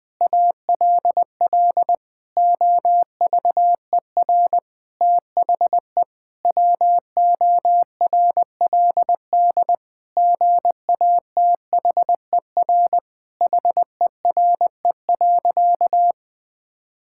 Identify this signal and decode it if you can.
Morse code, heard as a keyed tone, about 20 wpm, 700 Hz. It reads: ALL OVER THE WORLD GATHER HERE.
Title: Sentences with 6-Word Phrases - Spoken Beginning - 20wpm